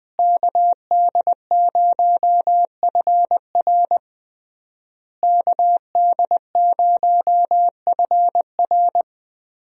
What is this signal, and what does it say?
Morse code, heard as a keyed tone, about 20 wpm, 700 Hz. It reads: KD0FR KD0FR
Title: USA Call Signs - 5 Characters - Repeated 3x - 20wpm